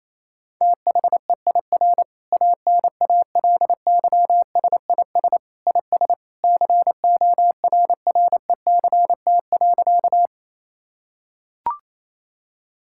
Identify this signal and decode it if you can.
Morse code, heard as a keyed tone, about 28 wpm, 700 Hz. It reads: THEIR ANALYSIS IS CORRECT.
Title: Sentences from Top 1000 Words - Review Part 5 - 28wpm